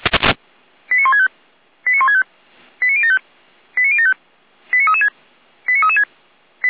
Five-tone selective call (ZVEI), heard as quick runs of five tones, tones of about 70 ms. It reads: 89165 89165 89075 89075 89336 89336